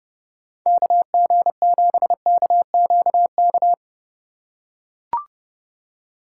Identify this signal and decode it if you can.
Morse code, heard as a keyed tone, about 30 wpm, 700 Hz. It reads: KG7KQK
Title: USA Call Signs - 6 Characters - 3x Character Spacing - 30wpm